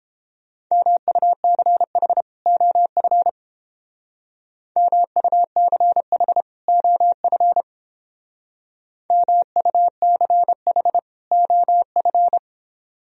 Morse code, heard as a keyed tone, about 29 wpm, 700 Hz. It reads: MUCH OF MUCH OF MUCH OF